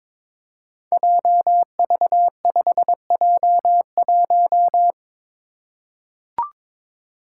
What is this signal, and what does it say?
Morse code, heard as a keyed tone, about 22 wpm, 700 Hz. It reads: JV5J1
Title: Five Challenging Letter-Number Combinations 22wpm